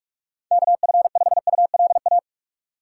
Morse code, heard as a keyed tone, about 46 wpm, 700 Hz. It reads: KF5ULA
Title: USA Call Signs - 6 Characters - 4x Character Spacing - 45wpm